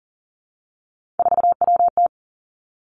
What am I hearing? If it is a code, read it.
Morse code, heard as a keyed tone, about 39 wpm, 700 Hz. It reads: 4WT
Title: Three Challenging Letter-Number Combinations 40wpm